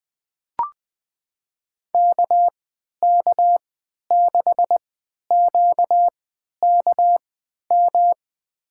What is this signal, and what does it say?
Morse code, heard as a keyed tone, about 20 wpm, spaced as for 11 wpm, 700 Hz. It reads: KK6QKM
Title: USA Call Signs - 6 Characters - 3x Character Spacing - 20wpm